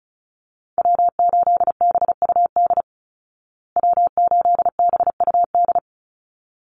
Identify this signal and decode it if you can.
Morse code, heard as a keyed tone, about 34 wpm, 700 Hz. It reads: W8BUD W8BUD
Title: Call Signs Repeated 3x - 35wpm